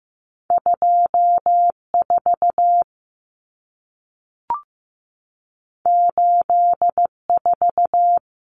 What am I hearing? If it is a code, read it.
Morse code, heard as a keyed tone, about 15 wpm, 700 Hz. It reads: 24 84
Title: Numbers - 2-Digit 15wpm